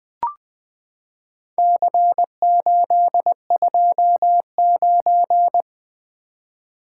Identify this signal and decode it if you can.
Morse code, heard as a keyed tone, about 20 wpm, 700 Hz. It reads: C829
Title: Four Letter-Number Combinations - 20wpm